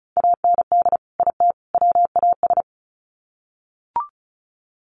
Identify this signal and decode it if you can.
Morse code, heard as a keyed tone, about 34 wpm, 700 Hz. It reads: AND IT WAS
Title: Sets of 3 Words 35wpm